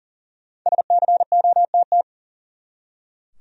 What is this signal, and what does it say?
Morse code, heard as a keyed tone, about 40 wpm, 700 Hz. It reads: SCOTT
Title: USA Common Names - Max 5 Letters - 40wpm